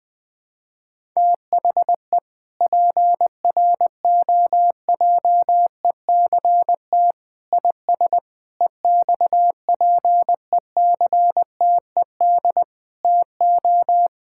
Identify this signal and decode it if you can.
Morse code, heard as a keyed tone, about 20 wpm, 700 Hz. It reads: THE PROJECT IS EXPECTED TO
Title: Sentences with 8-Word Phrases - 20wpm